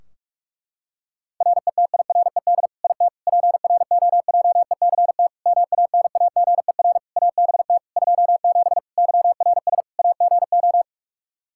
Morse code, heard as a keyed tone, about 45 wpm, 700 Hz. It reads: RETIRED IT PROJECT MANAGER ABT 27 YRS AGO
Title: Practice QSO - Exchange 3 - More Info - 45wpm